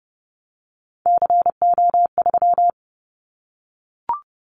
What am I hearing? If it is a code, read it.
Morse code, heard as a keyed tone, about 30 wpm, 700 Hz. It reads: CO3
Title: Callsign Prefixes for Top 100 DXCC - 30wpm